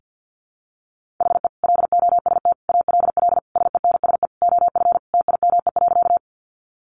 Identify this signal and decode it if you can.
Morse code, heard as a keyed tone, about 50 wpm, 700 Hz. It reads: HE LOST ALL SENSE OF TIME.
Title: Sentences from Top 1400 Words 50wpm